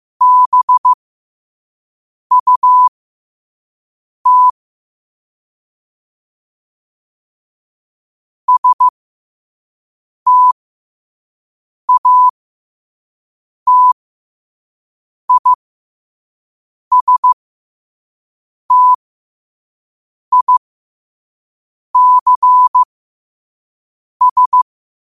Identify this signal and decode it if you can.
Morse code, heard as a keyed tone, about 15 wpm, spaced as for 5 wpm, 1000 Hz. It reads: BUT STATISTICS
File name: qotdfiles / 5wpm